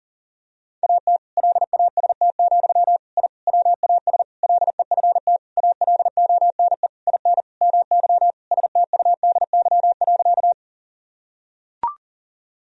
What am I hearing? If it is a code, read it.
Morse code, heard as a keyed tone, about 40 wpm, 700 Hz. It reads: AT LAST, I WAS LEFT ALONE IN MY STUDY.
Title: Sentences from Top 1000 Words - Review Part 1 - 40wpm